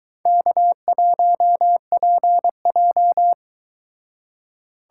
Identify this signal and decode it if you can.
Morse code, heard as a keyed tone, about 24 wpm, 700 Hz. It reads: K1PJ